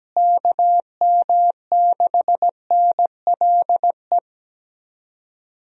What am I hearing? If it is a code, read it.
Morse code, heard as a keyed tone, about 17 wpm, 700 Hz. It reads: KM6NLE
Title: USA Call Signs - 6 Characters - Repeated 3x - 17wpm